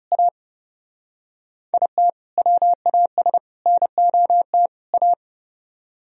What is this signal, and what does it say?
Morse code, heard as a keyed tone, about 36 wpm, 700 Hz. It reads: A IT WAS NOT A
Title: Speed-Racing - Sets of 4 Words 30wpm